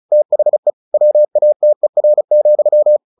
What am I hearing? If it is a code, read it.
Morse code, heard as a keyed tone, about 34 wpm, 600 Hz. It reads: THE WATER,
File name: the old man and the sea / the old man and the sea 1 35wpm